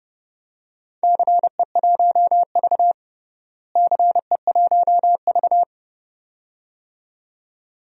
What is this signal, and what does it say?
Morse code, heard as a keyed tone, about 32 wpm, 700 Hz. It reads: CE1V CE1V
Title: DXCC-100 Call Signs - 4 Characters - Repeated 2x - 30wpm